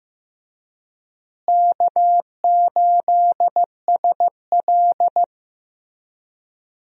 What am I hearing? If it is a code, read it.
Morse code, heard as a keyed tone, about 15 wpm, 700 Hz. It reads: K8SL
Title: USA Call Signs - 4 Characters - Repeated 2x - 15wpm